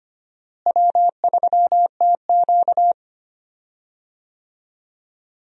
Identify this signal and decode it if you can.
Morse code, heard as a keyed tone, about 25 wpm, 700 Hz. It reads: W3TQ
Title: USA Call Signs - 4 Characters - Repeated 2x - 25wpm